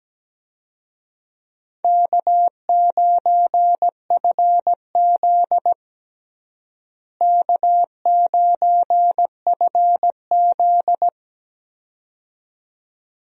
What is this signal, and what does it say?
Morse code, heard as a keyed tone, about 17 wpm, 700 Hz. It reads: K9FZ K9FZ
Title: USA Call Signs - 4 Characters - Repeated 2x - 17wpm